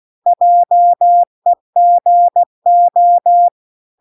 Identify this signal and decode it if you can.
Morse code, heard as a keyed tone, about 16 wpm, 700 Hz. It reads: JEGO